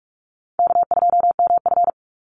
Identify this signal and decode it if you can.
Morse code, heard as a keyed tone, about 44 wpm, 700 Hz. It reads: K2MF
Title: Easy Call Signs Repeated 3x - 45wpm